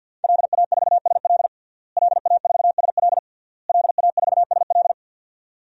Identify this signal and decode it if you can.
Morse code, heard as a keyed tone, about 51 wpm, 700 Hz. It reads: LA4SL LA4SL LA4SL
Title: DXCC-100 Call Signs - 5 Characters - Repeated 3x - 50wpm